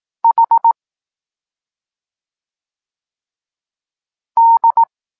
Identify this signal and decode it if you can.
Morse code, heard as a keyed tone, about 18 wpm, 920 Hz. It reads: H D
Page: Tekst 14 – doskonalenie odbioru poznanych liter